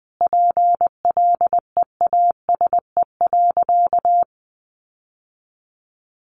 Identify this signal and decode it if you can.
Morse code, heard as a keyed tone, about 19 wpm, 700 Hz. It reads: PLEASE.
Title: Sentences from Top 1500 Words 20wpm